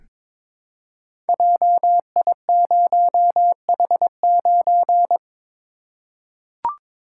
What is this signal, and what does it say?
Morse code, heard as a keyed tone, about 23 wpm, 700 Hz. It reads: JI0H9
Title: Code Groups - 5 Letter:Numbers - 22wpm